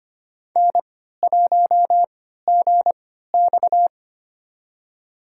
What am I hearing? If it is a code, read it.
Morse code, heard as a keyed tone, about 25 wpm, spaced as for 14 wpm, 700 Hz. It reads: N1GX
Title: USA Call Signs - 3x Character Spacing - 25wpm